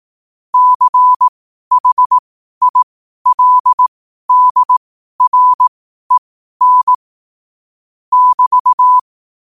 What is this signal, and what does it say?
Morse code, heard as a keyed tone, about 18 wpm, spaced as for 12 wpm, 1000 Hz. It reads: CHILDREN <BT>
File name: qotdfiles / 12wpm